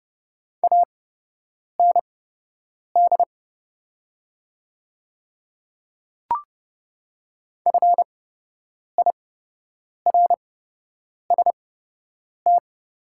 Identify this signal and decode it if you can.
Morse code, heard as a keyed tone, about 30 wpm, spaced as for 8 wpm, 700 Hz. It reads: AND FIRST
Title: 8x Character Spacing - Top 100 Words - 30wpm